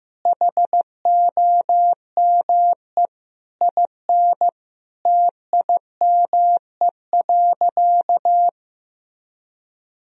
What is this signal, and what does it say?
Morse code, heard as a keyed tone, about 16 wpm, 700 Hz. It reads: HOME IN TIME.
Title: Sentences from Top 1000 Words - Review Part 3 - 15wpm